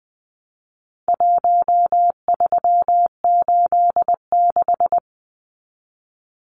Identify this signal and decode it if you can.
Morse code, heard as a keyed tone, about 20 wpm, 700 Hz. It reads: 1386
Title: Numbers - Contest Style - 4-Digit 20wpm